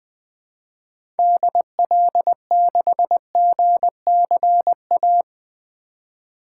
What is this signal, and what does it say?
Morse code, heard as a keyed tone, about 20 wpm, 700 Hz. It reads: DL6GCA
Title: 2023 SOTA Activator Call Signs - 20wpm